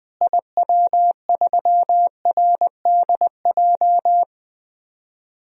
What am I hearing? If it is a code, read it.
Morse code, heard as a keyed tone, about 20 wpm, 700 Hz. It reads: IW3RDJ
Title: DXCC-100 Call Signs - 6 Characters - Repeated 2x - 20wpm